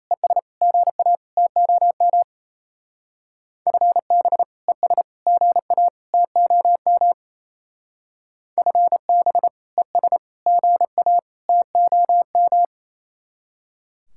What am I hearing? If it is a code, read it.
Morse code, heard as a keyed tone, about 34 wpm, 700 Hz. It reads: ES GA TOM FB ES GA TOM FB ES GA TOM
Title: Speed-Racing Common QSO Phrases Encore - 28wpm